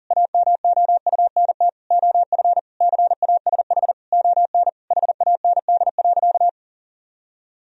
Morse code, heard as a keyed tone, about 40 wpm, 700 Hz. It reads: AMOUNT OF CASH ON HAND.
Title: Sentences from Top 1000 Words - Review Part 4 - 40wpm